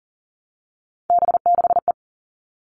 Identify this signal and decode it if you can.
Morse code, heard as a keyed tone, about 40 wpm, 700 Hz. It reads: B6E
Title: All Cut Numbers - 3-Digit 40wpm